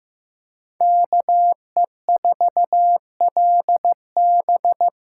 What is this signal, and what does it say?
Morse code, heard as a keyed tone, about 15 wpm, 700 Hz. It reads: KE4LB
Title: USA Call Signs - 8x Character Spacing - 15wpm